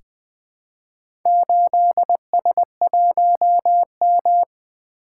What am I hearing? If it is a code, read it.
Morse code, heard as a keyed tone, about 20 wpm, 700 Hz. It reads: 8S1M
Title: Four Letter-Number Combinations - 20wpm